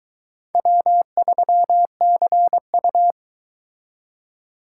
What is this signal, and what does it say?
Morse code, heard as a keyed tone, about 22 wpm, 700 Hz. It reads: W3CU